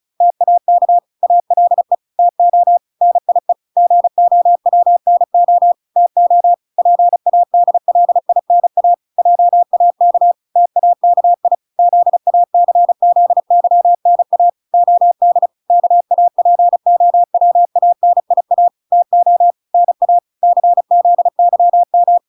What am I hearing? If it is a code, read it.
Morse code, heard as a keyed tone, about 34 wpm, 700 Hz. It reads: TAK ALE TO NIE GOWNO TO PADLINA JAK TAKI ZACZYNA OD KAPOWANIA TO NA CZYM